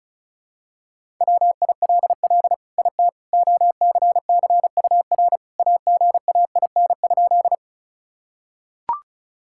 Morse code, heard as a keyed tone, about 36 wpm, 700 Hz. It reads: WILL IT OCCUR AGAIN?
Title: Sentences from Top 1000 Words - Review Part 6 - 35wpm